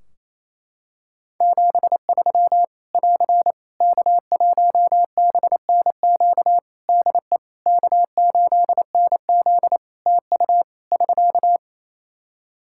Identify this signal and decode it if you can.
Morse code, heard as a keyed tone, about 28 wpm, 700 Hz. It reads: 73 <AR> K1BNQ DE K8NZ TU <SK>
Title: Practice QSO - Exchange 4 - Ending - 28wpm